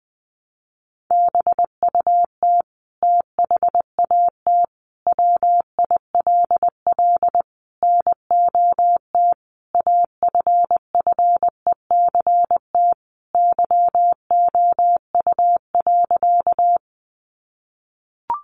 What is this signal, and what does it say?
Morse code, heard as a keyed tone, about 20 wpm, 700 Hz. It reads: BUT THAT WILL NOT AFFECT YOU.
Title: Sentences from Top 900 Words 20wpm